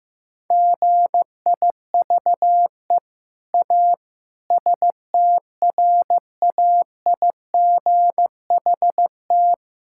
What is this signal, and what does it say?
Morse code, heard as a keyed tone, about 15 wpm, 700 Hz. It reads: GIVE A STRAIGHT
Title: Sentences from Top 1000 Words - Review Part 6 - 15wpm